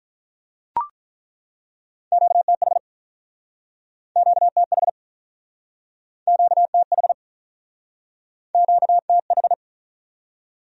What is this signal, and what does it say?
Morse code, heard as a keyed tone, about 49 wpm, 700 Hz. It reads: QTH QTH QTH QTH